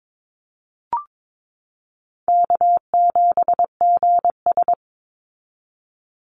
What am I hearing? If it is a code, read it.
Morse code, heard as a keyed tone, about 22 wpm, 700 Hz. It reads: K7GS
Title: Easy Call Signs Repeated 3x - 22wpm